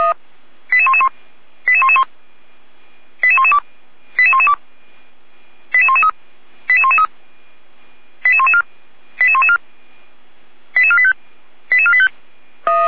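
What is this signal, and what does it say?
Five-tone selective call (ZVEI), heard as quick runs of five tones, tones of about 70 ms. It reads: 88111 88111 88112 88112 88113 88113 88114 88114 88485 88485
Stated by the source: ZVEI five-tone call received on SDR, Feuerwehr Orlamünde.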